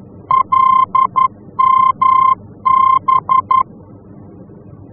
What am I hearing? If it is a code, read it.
Morse code, heard as a keyed tone, about 11 wpm, 1050 Hz. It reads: LMB
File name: france / mx / LMB-F-VOR